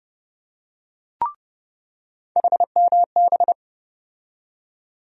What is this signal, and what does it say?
Morse code, heard as a keyed tone, about 30 wpm, 700 Hz. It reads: HMB
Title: USA Call Sign Postfixes - 3 Letters 30wpm